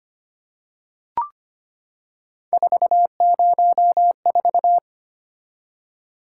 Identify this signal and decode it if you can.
Morse code, heard as a keyed tone, about 25 wpm, 700 Hz. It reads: 404
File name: Three Letter-Number Combinations - 25wpm